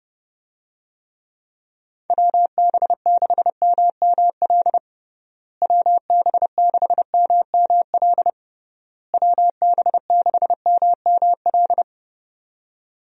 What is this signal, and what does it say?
Morse code, heard as a keyed tone, about 32 wpm, 700 Hz. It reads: WB6MML WB6MML WB6MML
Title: USA Call Signs - 6 Characters - Repeated 3x - 30wpm